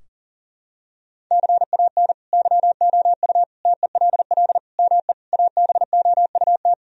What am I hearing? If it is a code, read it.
Morse code, heard as a keyed tone, about 40 wpm, 700 Hz. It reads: CAN YOU TELL ME ABOUT
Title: Sentences from Top 1000 Words - Review Part 3 - 40wpm